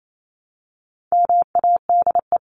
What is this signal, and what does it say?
Morse code, heard as a keyed tone, about 28 wpm, 700 Hz. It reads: MADE
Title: Top 300 Words Repeated 28wpm